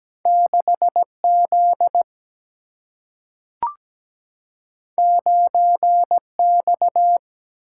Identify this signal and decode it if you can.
Morse code, heard as a keyed tone, about 17 wpm, 700 Hz. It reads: 6Z 9X
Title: Single-Letter Single-Number 17wpm